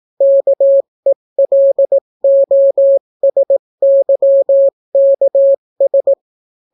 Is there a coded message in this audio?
Yes: Morse code from 0.2 to 6.1 s, heard as a keyed tone, about 18 wpm, 550 Hz. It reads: KELOSYKS